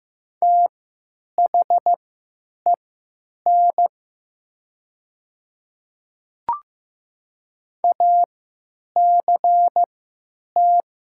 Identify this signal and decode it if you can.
Morse code, heard as a keyed tone, about 15 wpm, spaced as for 9 wpm, 700 Hz. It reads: THEN ACT
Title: 3x Character Spacing - Top 300 Words - 15wpm